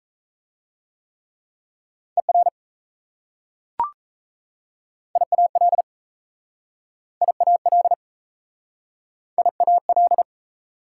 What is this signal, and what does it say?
Morse code, heard as a keyed tone, about 38 wpm, 690 Hz. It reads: ER IAL IAL IAL